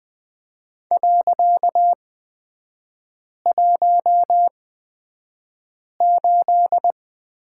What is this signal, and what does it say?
Morse code, heard as a keyed tone, about 20 wpm, spaced as for 5 wpm, 700 Hz. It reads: .18
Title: 255 Lesson 41e - Period - Three Characters 20wpm